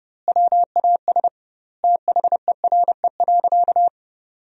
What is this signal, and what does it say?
Morse code, heard as a keyed tone, about 30 wpm, 700 Hz. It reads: WAS THERE.
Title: 2x Word-Spacing - Sentences from Top 500 Words - 30wpm